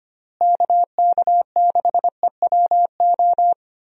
Morse code, heard as a keyed tone, about 25 wpm, 700 Hz. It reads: KK6EWO
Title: USA Call Signs - 6 Characters - Repeated 2x - 25wpm